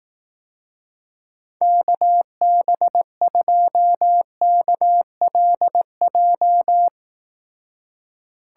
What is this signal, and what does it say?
Morse code, heard as a keyed tone, about 18 wpm, 700 Hz. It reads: KB2KLJ
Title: Speed Racing - USA 2x3 Call Signs 15wpm